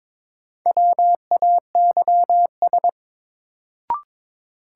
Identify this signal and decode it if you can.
Morse code, heard as a keyed tone, about 22 wpm, 700 Hz. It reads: WAYS
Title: Sets of 5 Words Encore - New Word Drill - 22wpm